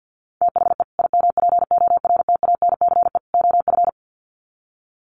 Morse code, heard as a keyed tone, about 50 wpm, 700 Hz. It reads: THE IMPORTANCE OF
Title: Sets of 3 Words 50wpm